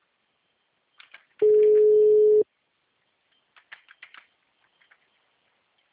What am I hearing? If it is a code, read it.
Ringback tone (Europe, 425 Hz).
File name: out- -2037- - - .28160